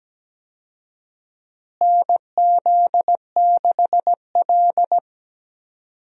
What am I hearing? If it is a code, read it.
Morse code, heard as a keyed tone, about 17 wpm, 700 Hz. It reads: NZ6L
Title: USA Call Signs - 4 Characters - Repeated 2x - 17wpm